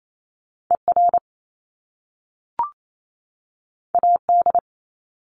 Morse code, heard as a keyed tone, about 28 wpm, 700 Hz. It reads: ER AD
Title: Top Two Letter Combinations in Words 28wpm